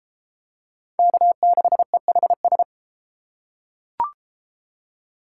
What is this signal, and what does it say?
Morse code, heard as a keyed tone, about 32 wpm, 700 Hz. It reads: K6EHS